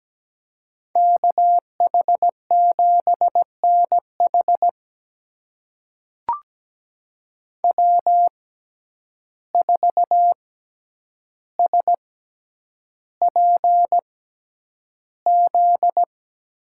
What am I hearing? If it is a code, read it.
Morse code, heard as a keyed tone, about 17 wpm, spaced as for 9 wpm, 700 Hz. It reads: KH7NH W4SPZ
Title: USA Call Signs - 6x Character Spacing - 17wpm